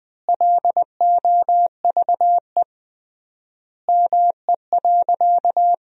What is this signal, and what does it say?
Morse code, heard as a keyed tone, about 21 wpm, 700 Hz. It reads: LOVE ME.
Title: 3x Word-Spacing - Sentences from Top 400 Words - 20wpm